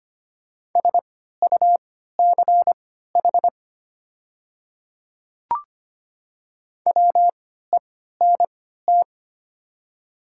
Morse code, heard as a keyed tone, about 25 wpm, spaced as for 14 wpm, 700 Hz. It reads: SUCH WENT
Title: 3x Character Spacing - Top 200 Words - 25wpm